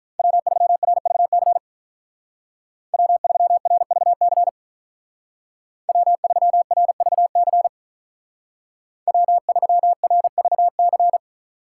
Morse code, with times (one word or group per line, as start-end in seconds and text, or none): 0.2-1.6 W3RVC
2.9-4.5 W3RVC
5.9-7.7 W3RVC
9.1-11.2 W3RVC